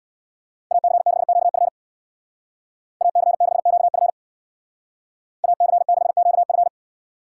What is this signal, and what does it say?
Morse code, heard as a keyed tone, about 58 wpm, 700 Hz. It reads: AC6QF AC6QF AC6QF
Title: Speed Racing - USA 2x2 Call Signs 50wpm